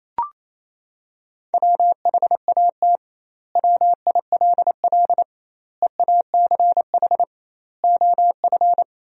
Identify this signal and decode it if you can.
Morse code, heard as a keyed tone, about 28 wpm, 700 Hz. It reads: WHAT WILL EACH OF
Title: 2x Word-Spacing - Sentences from Top 200 Words - 28wpm